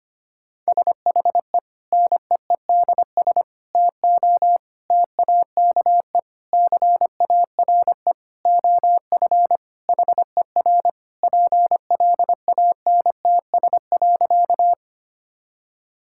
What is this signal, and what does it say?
Morse code, heard as a keyed tone, about 25 wpm, 700 Hz. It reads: SHE NEEDS TO TAKE CARE OF HER PLANTS.
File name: practice / 25wpm / Sentences with 4-Word Phrases - 25wpm